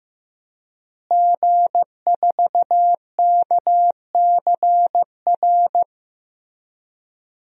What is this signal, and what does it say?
Morse code, heard as a keyed tone, about 15 wpm, 700 Hz. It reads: G4KCR
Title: DXCC-100 Call Signs - 5 Characters - Repeated 2x - 15wpm